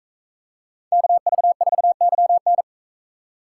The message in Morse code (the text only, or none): KV4YN